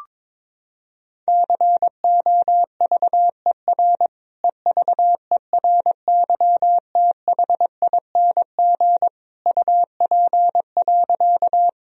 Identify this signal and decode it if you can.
Morse code, heard as a keyed tone, about 21 wpm, 700 Hz. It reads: COVER EVERYTHING UP.
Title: Sentences from Top 1000 Words - Review Part 1 - 22wpm